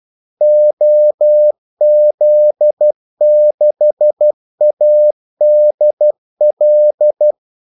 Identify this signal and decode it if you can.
Morse code, heard as a keyed tone, about 12 wpm, 600 Hz. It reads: OZ6ADL
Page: 2 ekstra ordmellemrum mellem kaldesignalerne